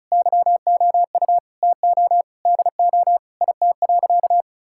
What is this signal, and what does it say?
Morse code, heard as a keyed tone, about 36 wpm, 700 Hz. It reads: YOU TO DO IT.
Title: Sentences from Top 1000 Words - Review Part 5 - 35wpm